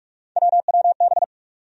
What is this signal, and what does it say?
Morse code, heard as a keyed tone, about 46 wpm, 700 Hz. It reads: WWB
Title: USA Call Sign Postfixes - 3 Letters 45wpm